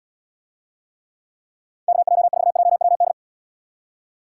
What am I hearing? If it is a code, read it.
Morse code, heard as a keyed tone, about 79 wpm, 700 Hz. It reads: KJ4JWD